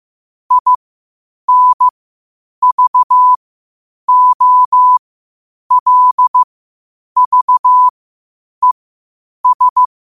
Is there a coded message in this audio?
Yes: Morse code from 0.5 to 9.8 s, heard as a keyed tone, about 15 wpm, spaced as for 8 wpm, 1000 Hz. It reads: INVOLVES